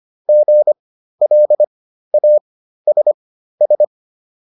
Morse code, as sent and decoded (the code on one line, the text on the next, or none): --. .-.. .- ... ...
GLASS